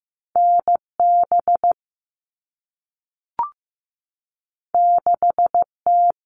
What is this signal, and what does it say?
Morse code, heard as a keyed tone, about 15 wpm, 700 Hz. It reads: NB 6T
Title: All Cut Numbers - 2-Digit 15wpm